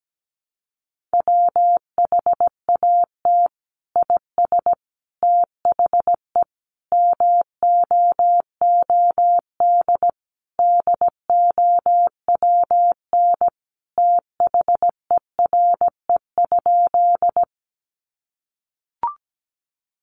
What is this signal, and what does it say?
Morse code, heard as a keyed tone, about 17 wpm, 700 Hz. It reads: WHAT IS THE MOOD DOWN THERE?
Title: Sentences from Top 1100 Words 17wpm